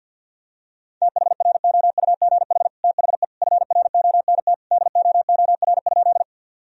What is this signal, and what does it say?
Morse code, heard as a keyed tone, about 50 wpm, 700 Hz. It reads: THROUGH THE FRONT DOOR?
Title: Sentences from Top 1000 Words - Review Part 2 - 50wpm